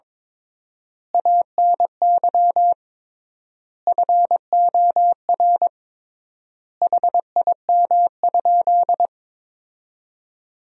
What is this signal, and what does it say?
Morse code, heard as a keyed tone, about 22 wpm, 700 Hz. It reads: ANY FOR HIM?
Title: 3x Word-Spacing - Sentences from Top 400 Words - 22wpm